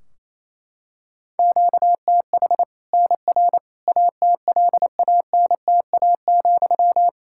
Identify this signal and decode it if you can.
Morse code, heard as a keyed tone, about 28 wpm, 700 Hz. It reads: QTH NR ATLANTA,
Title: Practice QSO - Exchange 1 - Beginning - 28wpm